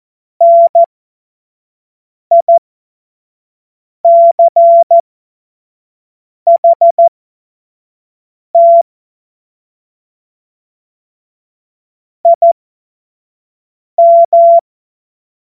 Morse code, heard as a keyed tone, about 14 wpm, spaced as for 5 wpm, 680 Hz. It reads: NICHT IM
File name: Morsekurs / Witze / Witz 08. August 2023 25 F70